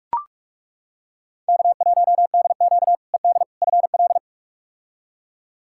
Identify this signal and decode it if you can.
Morse code, heard as a keyed tone, about 42 wpm, 700 Hz. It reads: K1DQ ED FL
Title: K1USN SST QSO Practice - 45wpm